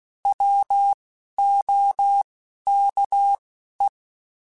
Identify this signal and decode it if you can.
Morse code, heard as a keyed tone, about 16 wpm, 800 Hz. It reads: W O K E
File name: Alphabetical / Alphabetical mp3s / Ws @10